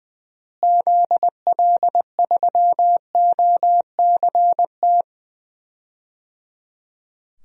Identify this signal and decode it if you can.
Morse code, heard as a keyed tone, about 20 wpm, 700 Hz. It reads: ZL3OCT
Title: DXCC-100 Call Signs - 6 Characters - Repeated 3x - 20wpm